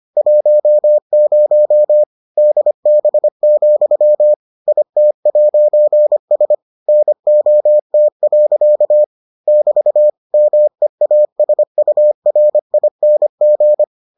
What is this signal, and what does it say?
Morse code, heard as a keyed tone, about 25 wpm, 600 Hz. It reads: 10 DB, IT'S NOT. <BT> MEASURING